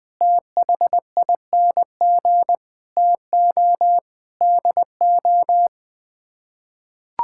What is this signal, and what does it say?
Morse code, heard as a keyed tone, about 20 wpm, 700 Hz. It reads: THING TO DO
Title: Sets of 6 Words - 20wpm